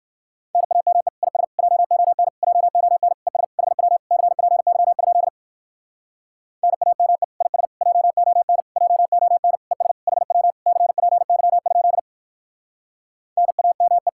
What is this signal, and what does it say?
Morse code, heard as a keyed tone, about 51 wpm, 700 Hz. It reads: NAME IS JON JON ES HW CPY? NAME IS JON JON ES HW CPY? NAME
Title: Speed-Racing Common QSO Phrases Encore - 45wpm